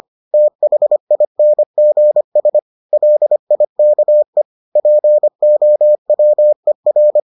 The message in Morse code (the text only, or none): THINGS LIKE POWER